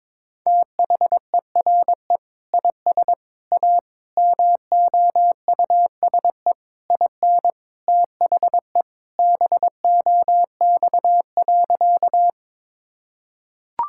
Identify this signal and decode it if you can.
Morse code, heard as a keyed tone, about 22 wpm, 700 Hz. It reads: THERE IS A MOUSE IN THE BOX.
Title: Sentences from Top 1000 Words - Review Part 5 - 22wpm